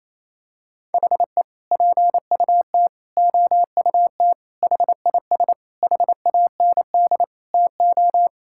Morse code, heard as a keyed tone, about 28 wpm, 700 Hz. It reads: HE PUT OUT HIS HAND TO
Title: Sentences from Top 1000 Words - Review Part 5 - 28wpm